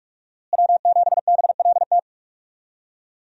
Morse code, heard as a keyed tone, about 42 wpm, 700 Hz. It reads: W7BLT
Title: USA Call Signs - 5 Characters - Repeated 2x - 45wpm